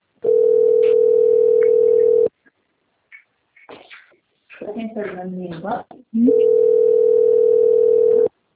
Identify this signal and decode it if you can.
Ringback tone (North America), 440 + 480 Hz, with speech.
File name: rg-1600-unknown- - - .17056